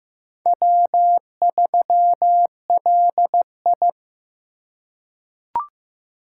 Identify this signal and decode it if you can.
Morse code, heard as a keyed tone, about 15 wpm, 700 Hz. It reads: W3LI
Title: USA Call Signs - 4 Characters - 4x Character Spacing - 15wpm